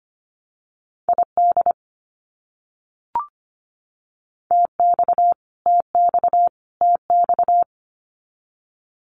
Morse code, heard as a keyed tone, about 25 wpm, 700 Hz. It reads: ID TX TX TX
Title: US States Repeated 3x 25wpm